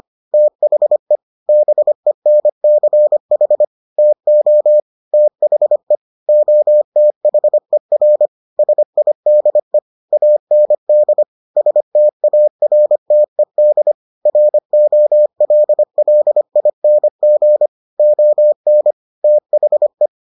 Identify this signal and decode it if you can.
Morse code, heard as a keyed tone, about 25 wpm, 600 Hz. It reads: THE BENCH TO THE OTHER SIDE AND STARTED ROLLING ON THE